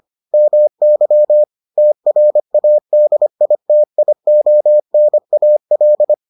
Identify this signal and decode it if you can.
Morse code, heard as a keyed tone, about 25 wpm, 600 Hz. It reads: MY TRADITIONAL